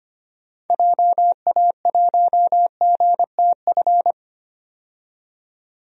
Morse code, heard as a keyed tone, about 25 wpm, 700 Hz. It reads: JA1GTF
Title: DXCC-100 Call Signs - 6 Characters - 25wpm